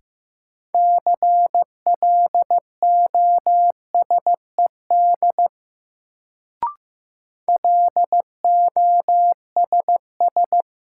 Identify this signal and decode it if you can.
Morse code, heard as a keyed tone, about 15 wpm, 700 Hz. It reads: CLOSED LOSS
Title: Sentences from Top 1400 Words - New Word Drill Alt - 15wpm